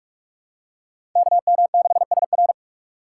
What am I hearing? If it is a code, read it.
Morse code, heard as a keyed tone, about 42 wpm, 700 Hz. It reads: KM6SR
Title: USA Call Signs - 5 Characters - 4x Character Spacing - 45wpm